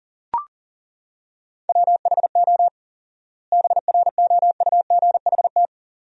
Morse code, heard as a keyed tone, about 40 wpm, 700 Hz. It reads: WHO BROUGHT
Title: 4x Word-Spacing - Sentences from Top 500 Words - 40wpm